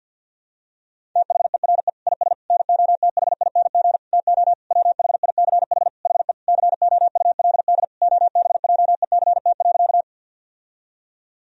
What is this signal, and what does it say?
Morse code, heard as a keyed tone, about 50 wpm, 700 Hz. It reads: THERE IS NOTHING TO WHICH HE COULD OBJECT.
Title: Sentences from Top 1000 Words - Review Part 3 - 50wpm